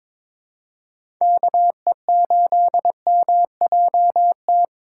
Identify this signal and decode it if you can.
Morse code, heard as a keyed tone, about 22 wpm, 700 Hz. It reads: KE8MJT
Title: USA Call Signs - 6 Characters - 3x Character Spacing - 22wpm